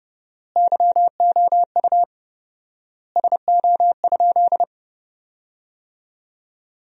Morse code, heard as a keyed tone, about 30 wpm, 700 Hz. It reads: YOU SO?
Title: 4x Word-Spacing - Sentences from Top 400 Words - 30wpm